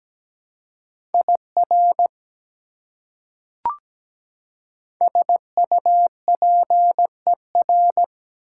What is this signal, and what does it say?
Morse code, heard as a keyed tone, about 17 wpm, 700 Hz. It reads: IR SUPER
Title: Prefixes and Suffixes 17wpm